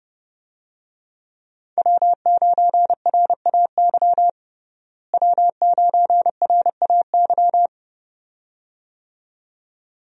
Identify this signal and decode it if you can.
Morse code, heard as a keyed tone, about 32 wpm, 700 Hz. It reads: W9RAY W9RAY
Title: USA Call Signs - 5 Characters - Repeated 2x - 30wpm